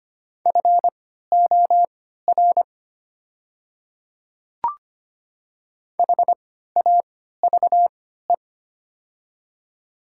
Morse code, heard as a keyed tone, about 25 wpm, spaced as for 14 wpm, 690 Hz. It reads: FOR HAVE
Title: 3x Character Spacing - Top 200 Words - 25wpm